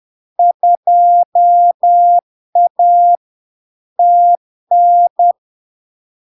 Morse code, heard as a keyed tone, about 10 wpm, 700 Hz. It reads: 2A TN
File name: LARK-FD-2022-Log-in-CW-10wpm-700-Hz